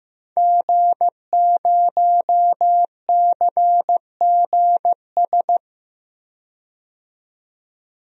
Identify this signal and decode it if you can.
Morse code, heard as a keyed tone, about 15 wpm, 700 Hz. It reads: G0CGS
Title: DXCC-100 Call Signs - 5 Characters - 15wpm